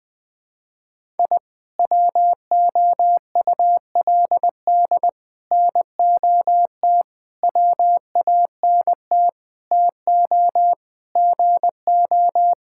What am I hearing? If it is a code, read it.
Morse code, heard as a keyed tone, about 20 wpm, 700 Hz. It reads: I WOULD NOT WANT TO GO
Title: Sentences with 5-Word Phrases - Spoken Beginning - 20wpm